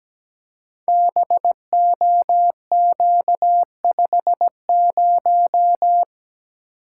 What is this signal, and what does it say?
Morse code, heard as a keyed tone, about 17 wpm, 700 Hz. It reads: BOQ50
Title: Code Groups - 5 Letter:Numbers - 17wpm